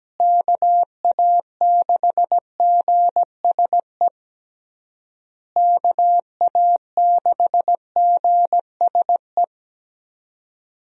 Morse code, heard as a keyed tone, about 17 wpm, 700 Hz. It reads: KA6GSE KA6GSE
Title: USA Call Signs - 6 Characters - Repeated 3x - 17wpm